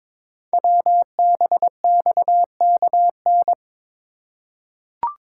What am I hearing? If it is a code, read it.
Morse code, heard as a keyed tone, about 22 wpm, 690 Hz. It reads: WBXKN
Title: Code Groups - 5 Letters - 22wpm